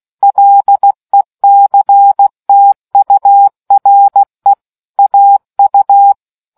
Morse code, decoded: LECTURE AU